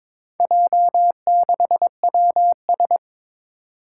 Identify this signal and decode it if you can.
Morse code, heard as a keyed tone, about 22 wpm, 700 Hz. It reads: J6WS
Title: Four Challenging Letter-Number Combinations 22wpm